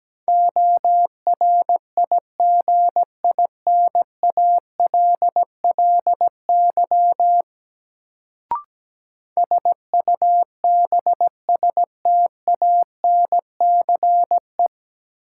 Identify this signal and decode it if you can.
Morse code, heard as a keyed tone, about 17 wpm, 700 Hz. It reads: ORIGINALLY SUBSTANCE
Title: Sentences from Top 1500 Words - New Word Drill - 17wpm